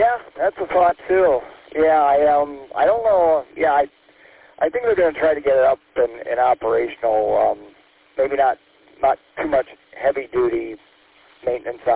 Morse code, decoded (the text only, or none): none